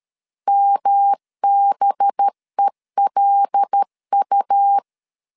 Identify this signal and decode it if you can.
Morse code, heard as a keyed tone, about 13 wpm, 800 Hz. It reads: MBELU
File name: LEZIONI / 6 PKX0 / 6 P 12wpm 2033